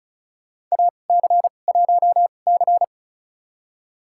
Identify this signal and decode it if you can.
Morse code, heard as a keyed tone, about 34 wpm, spaced as for 25 wpm, 700 Hz. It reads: AC1C
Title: USA Call Signs - 4 Characters - 2x Character Spacing - 35wpm